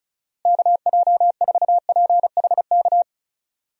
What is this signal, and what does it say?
Morse code, heard as a keyed tone, about 34 wpm, 700 Hz. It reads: KJ4PHK
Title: USA Call Signs - 6 Characters - Repeated 3x - 35wpm